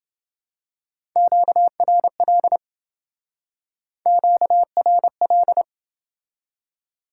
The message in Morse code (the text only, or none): QRL QRL